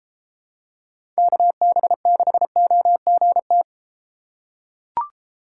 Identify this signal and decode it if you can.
Morse code, heard as a keyed tone, about 32 wpm, 700 Hz. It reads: KB6OGT